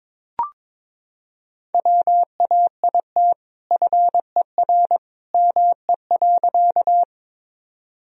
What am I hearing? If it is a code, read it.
Morse code, heard as a keyed tone, about 21 wpm, 700 Hz. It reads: WAIT FER ME.
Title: Sentences from Top 500 Words and Abbreviations 22wpm